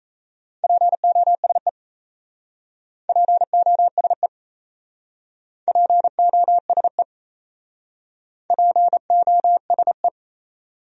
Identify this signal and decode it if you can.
Morse code, heard as a keyed tone, about 36 wpm, 700 Hz. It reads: POSE POSE POSE POSE